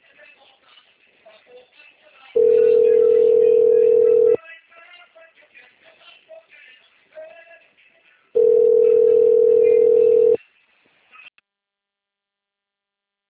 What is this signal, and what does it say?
Ringback tone (North America), 440 + 480 Hz.